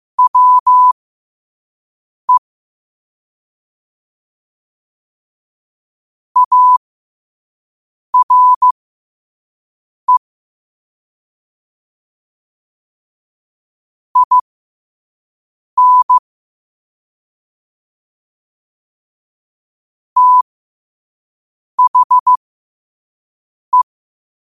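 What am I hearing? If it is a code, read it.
Morse code, heard as a keyed tone, about 15 wpm, spaced as for 5 wpm, 1000 Hz. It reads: WE ARE IN THE
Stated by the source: Morse code sent at 5 wpm.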